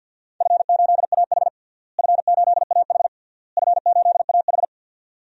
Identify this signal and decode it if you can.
Morse code, heard as a keyed tone, about 49 wpm, 700 Hz. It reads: F8AH F8AH F8AH
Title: DXCC-100 Call Signs - 4 Characters - Repeated 3x - 50wpm